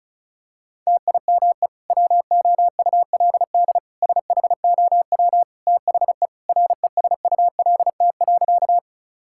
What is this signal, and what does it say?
Morse code, heard as a keyed tone, about 36 wpm, 700 Hz. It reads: TIME WOULD SHOW THE RESULT.
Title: Sentences from Top 1000 Words - Review Part 5 - 35wpm